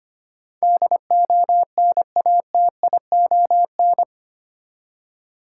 Morse code, heard as a keyed tone, about 25 wpm, 690 Hz. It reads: DONATION
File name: 8 Letter Words - First 3 Letters Given - 25wpm